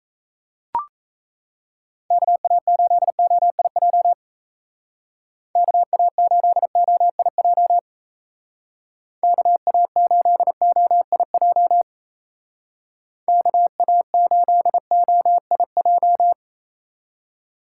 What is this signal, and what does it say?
Morse code, heard as a keyed tone, about 35 wpm, 700 Hz. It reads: KA8OIJ KA8OIJ KA8OIJ KA8OIJ
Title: Speed Racing - USA 2x3 Call Signs 28wpm